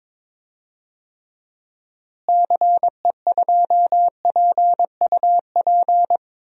Morse code, heard as a keyed tone, about 22 wpm, 700 Hz. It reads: CE2PUP
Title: DXCC-100 Call Signs - 6 Characters - Repeated 3x - 22wpm